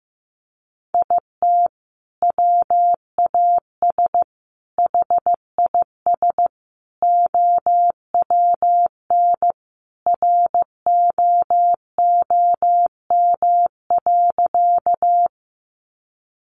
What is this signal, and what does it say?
Morse code, heard as a keyed tone, about 15 wpm, 700 Hz. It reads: IT WAS HIS OWN ROOM.
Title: Sentences from Top 300 Words 15wpm